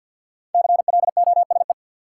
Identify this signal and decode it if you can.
Morse code, heard as a keyed tone, about 53 wpm, 700 Hz. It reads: CLOSE
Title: Sentences from Top 300 Words - New Word Drill - 50wpm